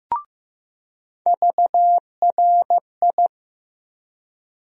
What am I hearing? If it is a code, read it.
Morse code, heard as a keyed tone, about 15 wpm, 700 Hz. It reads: VRI
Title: USA Call Sign Postfixes - 3 Letters 15wpm